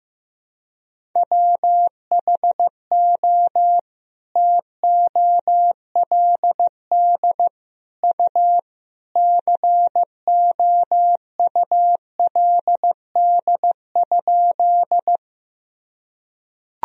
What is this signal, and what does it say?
Morse code, heard as a keyed tone, about 16 wpm, 700 Hz. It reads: WHO TOLD U COULD?
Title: Sentences from Top 400 Words and Abbreviations 15wpm